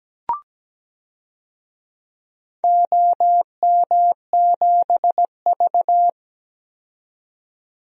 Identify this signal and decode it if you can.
Morse code, heard as a keyed tone, about 17 wpm, 700 Hz. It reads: OM7V
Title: DXCC-100 Call Signs - 4 Characters - 17wpm